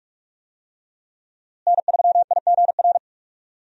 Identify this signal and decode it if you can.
Morse code, heard as a keyed tone, about 46 wpm, 700 Hz. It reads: N3IGR
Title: USA Call Signs - 5 Characters - Repeated 3x - 45wpm